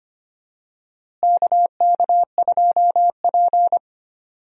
Morse code, heard as a keyed tone, about 25 wpm, 700 Hz. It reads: KK2P
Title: USA Call Signs - 4 Characters - Repeated 3x - 25wpm